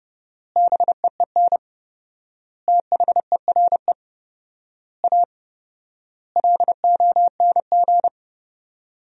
Morse code, heard as a keyed tone, about 30 wpm, 700 Hz. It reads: BEEN THERE A LONG
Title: 4x Word-Spacing - Sentences from Top 400 Words - 30wpm